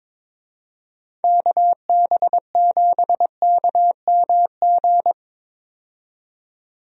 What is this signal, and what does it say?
Morse code, heard as a keyed tone, about 22 wpm, 700 Hz. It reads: KB7KMG
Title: USA Call Signs - 6 Characters - 22wpm